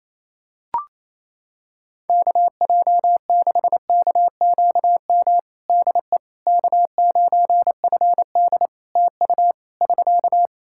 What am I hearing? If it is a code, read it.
Morse code, heard as a keyed tone, about 28 wpm, 700 Hz. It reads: KJ6KQM DE K9FD TU <SK>
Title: K1USN SST QSO Practice 28wpm